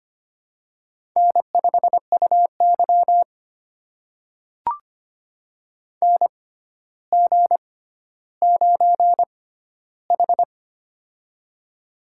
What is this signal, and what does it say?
Morse code, heard as a keyed tone, about 25 wpm, spaced as for 13 wpm, 700 Hz. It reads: N5UY NG9H
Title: USA Call Signs - 4 Characters - 6x Character Spacing - 25wpm